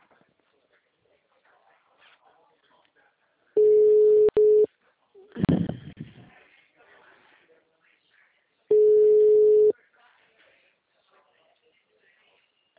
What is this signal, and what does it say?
Ringback tone (Europe, 425 Hz), with speech.
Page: Atibaia - SP